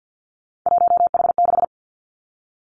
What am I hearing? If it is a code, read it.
Morse code, heard as a keyed tone, about 50 wpm, 700 Hz. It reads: 1H6
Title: Three Challenging Letter-Number Combinations 50wpm